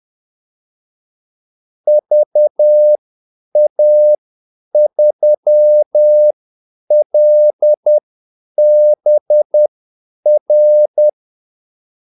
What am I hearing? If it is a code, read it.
Morse code, heard as a keyed tone, about 10 wpm, spaced as for 8 wpm, 600 Hz. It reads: VA3LBR
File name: Callsigns-10-8wpm